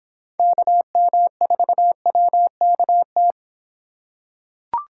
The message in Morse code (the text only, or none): KM4WKT